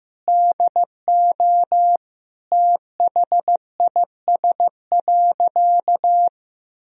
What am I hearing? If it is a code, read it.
Morse code, heard as a keyed tone, about 15 wpm, 700 Hz. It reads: DO THIS.
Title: Sentences from Top 1000 Words - Review Part 3 - 15wpm